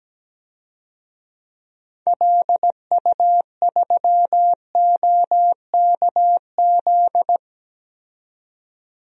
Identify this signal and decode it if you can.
Morse code, heard as a keyed tone, about 17 wpm, 700 Hz. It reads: LU3OKZ
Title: DXCC-100 Call Signs - 6 Characters - 17wpm